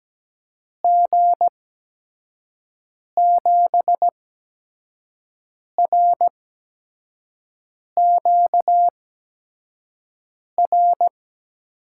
Morse code, heard as a keyed tone, about 17 wpm, spaced as for 5 wpm, 700 Hz. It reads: G7RQR
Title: USA Call Signs - 6 Characters - 8x Character Spacing - 17wpm